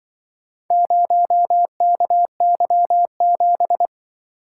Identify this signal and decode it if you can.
Morse code, heard as a keyed tone, about 24 wpm, 700 Hz. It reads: 0KY7
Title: Speed-Racing - Four Letter-Number Combinations - 20wpm